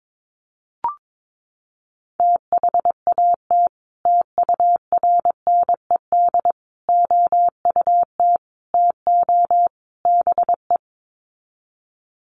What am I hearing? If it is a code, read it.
Morse code, heard as a keyed tone, about 22 wpm, 700 Hz. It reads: THAT TURNED OUT TO BE
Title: Sets of 5 Words Encore2 22wpm